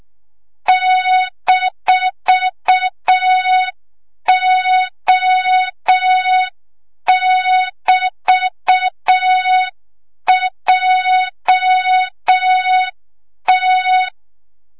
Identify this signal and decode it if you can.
Morse code, heard as a keyed tone, about 6 wpm, 3700 Hz, with noise. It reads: -O<BT>JT
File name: Morze-HA2MN / hang / 25 GYAKORLAT